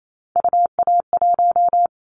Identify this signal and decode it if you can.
Morse code, heard as a keyed tone, about 28 wpm, 700 Hz. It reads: UA1
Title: Callsign Prefixes for Top 25 DXCC - 28wpm